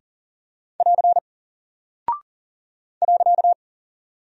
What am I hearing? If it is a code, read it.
Morse code, heard as a keyed tone, about 40 wpm, 700 Hz. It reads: <AR> .
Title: Single Character (Letter:Number:Punctuation:Prosign) Alt 40wpm